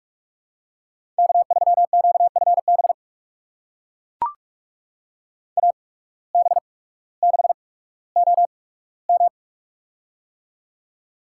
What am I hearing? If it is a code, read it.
Morse code, heard as a keyed tone, about 42 wpm, spaced as for 19 wpm, 700 Hz. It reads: K3QFB AB6OM
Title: USA Call Signs - 5 Characters - 8x Character Spacing - 45wpm